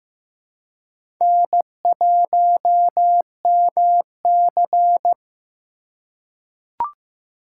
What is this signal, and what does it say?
Morse code, heard as a keyed tone, about 15 wpm, 700 Hz. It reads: N1MC
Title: USA Call Signs - 4x Character Spacing - 15wpm